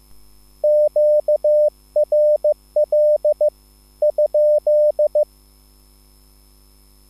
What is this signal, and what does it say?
Morse code, heard as a keyed tone, about 15 wpm, spaced as for 13 wpm, 600 Hz. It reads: QRL?
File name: QSL / CW lesson